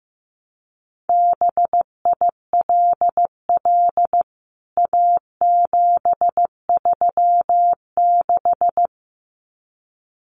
Morse code, heard as a keyed tone, about 15 wpm, 700 Hz. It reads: BILL A736
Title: CWT Exchange 15wpm